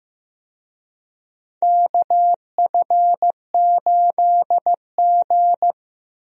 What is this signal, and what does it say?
Morse code, heard as a keyed tone, about 15 wpm, 700 Hz. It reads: KF8G
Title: USA Call Signs - 4 Characters - Repeated 2x - 15wpm